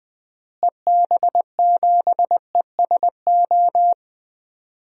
Morse code, heard as a keyed tone, about 20 wpm, 690 Hz. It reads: EB7ESO
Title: DXCC-100 Call Signs - 6 Characters - Repeated 2x - 20wpm